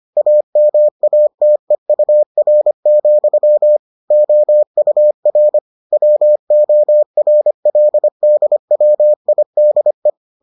Morse code, heard as a keyed tone, about 25 wpm, 600 Hz. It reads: AMATEUR, OUR WORLDWIDE